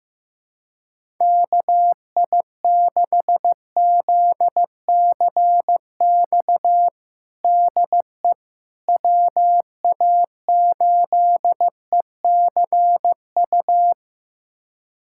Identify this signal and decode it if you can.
Morse code, heard as a keyed tone, about 16 wpm, 700 Hz. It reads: KI6ZCX DE WA8ECU
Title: USA Callsign DE Callsign - 6 Characters - 15wpm